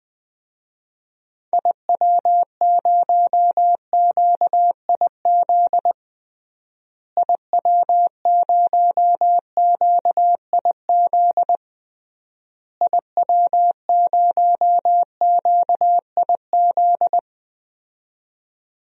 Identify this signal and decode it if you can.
Morse code, heard as a keyed tone, about 20 wpm, 700 Hz. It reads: IW0QIZ IW0QIZ IW0QIZ
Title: DXCC-100 Call Signs - 6 Characters - Repeated 3x - 20wpm